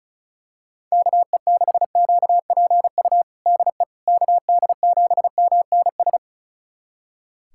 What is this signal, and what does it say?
Morse code, heard as a keyed tone, about 36 wpm, 700 Hz. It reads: KE6QPU DE KD7MNS
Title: USA Callsign DE Callsign 35wpm